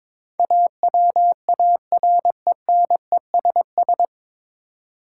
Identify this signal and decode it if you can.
Morse code, heard as a keyed tone, about 22 wpm, 700 Hz. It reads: AWARENESS
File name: 9 Letter Words - First 3 Letters Given - 22wpm